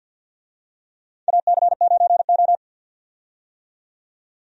Morse code, heard as a keyed tone, about 48 wpm, 700 Hz. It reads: AC9O
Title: USA Call Signs - 4 Characters - 50wpm